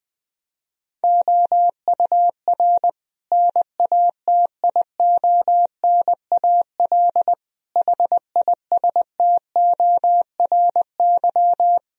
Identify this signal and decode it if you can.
Morse code, heard as a keyed tone, about 19 wpm, 700 Hz. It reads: OUR NATIONAL HISTORY
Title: Sentences from Top 1000 Words - Review Part 4 - 20wpm